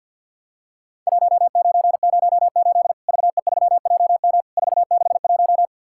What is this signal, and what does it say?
Morse code, heard as a keyed tone, about 50 wpm, 700 Hz. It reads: 1908 VE3JM 461
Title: Worked All Europe DX Contest - QTC Exchange - 50wpm